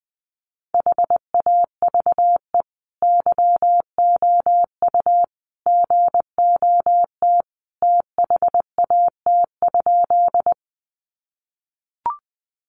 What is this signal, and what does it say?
Morse code, heard as a keyed tone, about 20 wpm, 700 Hz. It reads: HAVE YOU GOT THAT?
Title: Sentences from Top 300 Words 20wpm